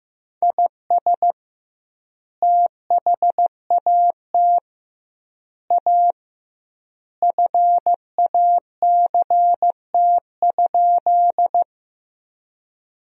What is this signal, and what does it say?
Morse code, heard as a keyed tone, about 15 wpm, 700 Hz. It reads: IS THAT A FACT?
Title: 2x Word-Spacing - Sentences from Top 500 Words - 15wpm